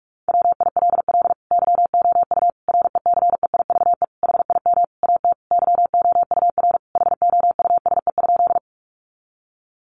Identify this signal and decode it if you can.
Morse code, heard as a keyed tone, about 44 wpm, 700 Hz. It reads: WILL YOU RECEIVE HIM AT YOUR HOUSE?
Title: Sentences from Top 1000 Words 45wpm